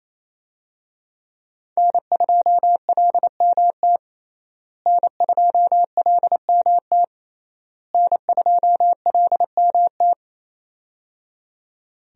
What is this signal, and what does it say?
Morse code, heard as a keyed tone, about 28 wpm, 700 Hz. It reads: N2LMT N2LMT N2LMT
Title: USA Call Signs - 5 Characters - Repeated 3x - 28wpm